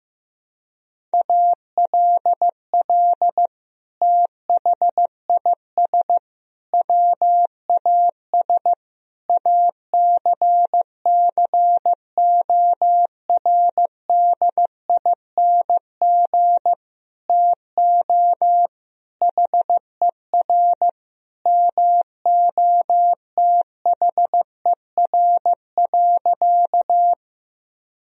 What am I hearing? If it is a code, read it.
Morse code, heard as a keyed tone, about 16 wpm, 700 Hz. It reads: ALL THIS WAS ACCORDING TO HER MOTHER.
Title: Sentences from Top 1000 Words - Review Part 4 - 15wpm